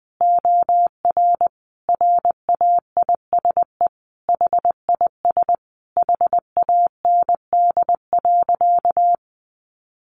Morse code, heard as a keyed tone, about 20 wpm, 700 Hz. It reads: OR RAISE HIS HAND.
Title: Sentences from Top 1400 Words 20wpm